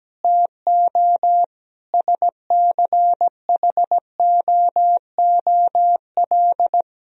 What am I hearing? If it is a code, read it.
Morse code, heard as a keyed tone, about 17 wpm, 700 Hz. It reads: TO SCHOOL
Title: Sentences from Top 1000 Words - Review Part 4 - 17wpm